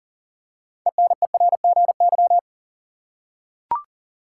Morse code, heard as a keyed tone, about 40 wpm, 700 Hz. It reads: ENERGY
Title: Six Letter Words - 40wpm